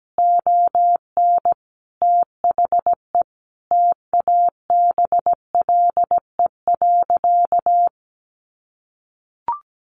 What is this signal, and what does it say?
Morse code, heard as a keyed tone, about 17 wpm, 700 Hz. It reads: ON THE TABLE.
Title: Sentences from Top 1200 Words 17wpm